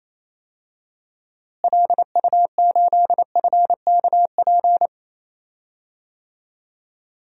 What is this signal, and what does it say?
Morse code, heard as a keyed tone, about 28 wpm, 700 Hz. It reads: LU8FKP
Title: DXCC-100 Call Signs - 6 Characters - Repeated 3x - 28wpm